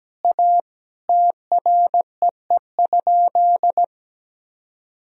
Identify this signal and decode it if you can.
Morse code, heard as a keyed tone, about 17 wpm, 700 Hz. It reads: A TREE?
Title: Sentences from Top 1000 Words - Review Part 2 - 17wpm